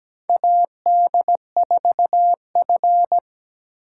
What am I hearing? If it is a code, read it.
Morse code, heard as a keyed tone, about 17 wpm, 700 Hz. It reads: AD4F
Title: USA Call Signs - 4 Characters - 8x Character Spacing - 17wpm